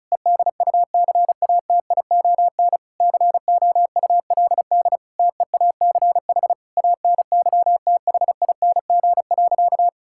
Morse code, heard as a keyed tone, about 34 wpm, 700 Hz. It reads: EDUCATION COULD TEACH ANYTHING.
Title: Sentences from Top 1000 Words - Review Part 6 - 35wpm